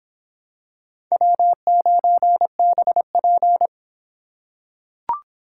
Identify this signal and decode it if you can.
Morse code, heard as a keyed tone, about 26 wpm, 700 Hz. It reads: W9BP